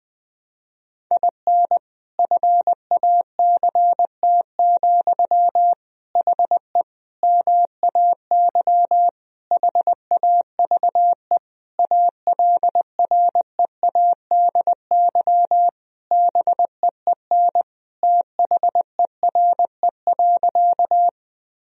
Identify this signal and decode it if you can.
Morse code, heard as a keyed tone, about 20 wpm, 700 Hz. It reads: IN FACT, HE MAY HAVE ALREADY BEEN THERE.
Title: Sentences from Top 1000 Words - Review Part 3 - 20wpm